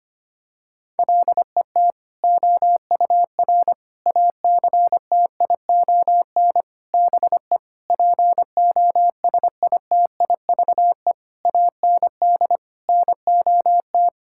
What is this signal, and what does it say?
Morse code, heard as a keyed tone, about 25 wpm, 700 Hz. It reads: LET OUR ACTION BE POSITIVE AND NOT
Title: Sentences from Top 1000 Words - Review Part 6 - 25wpm